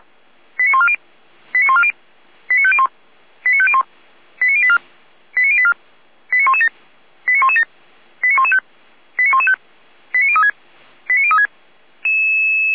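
Five-tone selective call (ZVEI), heard as quick runs of five tones, tones of about 70 ms. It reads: 89140 89140 89501 89501 89084 89084 89117 89117 89115 89115 89036 89036
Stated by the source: Five-tone ZVEI selective call received on an SDR.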